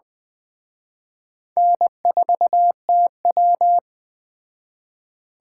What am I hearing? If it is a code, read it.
Morse code, heard as a keyed tone, about 20 wpm, 700 Hz. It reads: N4TW
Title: Speed Racing - USA 1x2 Call Signs 17wpm